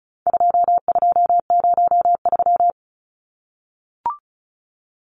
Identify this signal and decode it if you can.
Morse code, heard as a keyed tone, about 34 wpm, 700 Hz. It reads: 2203
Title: Numbers - 4-Digit 35wpm